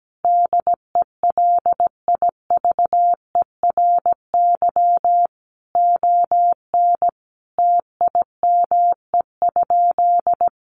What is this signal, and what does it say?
Morse code, heard as a keyed tone, about 17 wpm, 700 Hz. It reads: DELIVERY ON TIME?
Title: Sentences from Top 1500 Words 17wpm